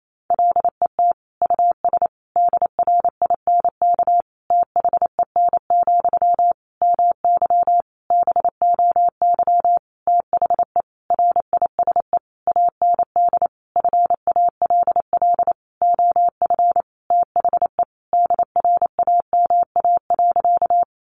Morse code, heard as a keyed tone, about 28 wpm, 700 Hz. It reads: LET US DRINK THEN, MY BOY THE RISE AND FALL OF THE DRAMA.
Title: Sentences from Top 1500 Words 28wpm